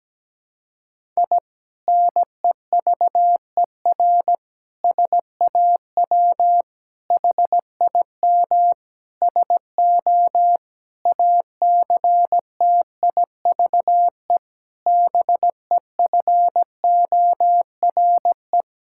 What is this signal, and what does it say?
Morse code, heard as a keyed tone, about 17 wpm, 700 Hz. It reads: I NEVER SAW HIM SO ACTIVE BEFORE
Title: Sentences from Top 1000 Words - Review Part 6 - 17wpm